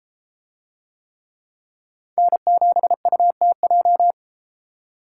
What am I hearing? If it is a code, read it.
Morse code, heard as a keyed tone, about 33 wpm, 700 Hz. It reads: N7UTJ